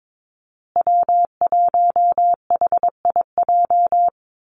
Morse code, heard as a keyed tone, about 22 wpm, 700 Hz. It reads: W1HIJ
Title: Call Signs Repeated 2x - 22wpm